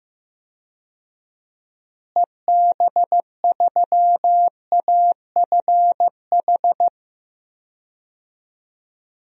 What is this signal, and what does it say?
Morse code, heard as a keyed tone, about 15 wpm, 700 Hz. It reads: EB3AFH
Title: DXCC-100 Call Signs - 6 Characters - Repeated 2x - 15wpm